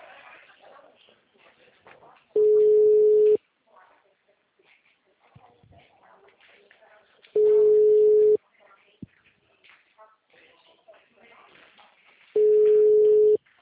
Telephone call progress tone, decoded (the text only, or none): ringback tone